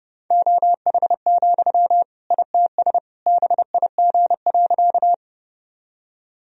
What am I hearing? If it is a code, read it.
Morse code, heard as a keyed tone, about 30 wpm, 700 Hz. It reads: OH, ITS BIG.
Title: Sentences from Top 1000 Words - Review Part 1 - 30wpm